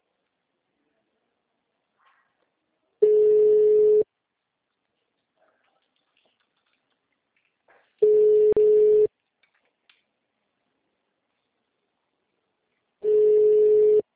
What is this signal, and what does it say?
Ringback tone (Europe, 425 Hz).